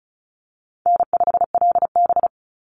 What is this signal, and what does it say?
Morse code, heard as a keyed tone, about 34 wpm, 700 Hz. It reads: N5LB
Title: Easy Call Signs Repeated 2x - 35wpm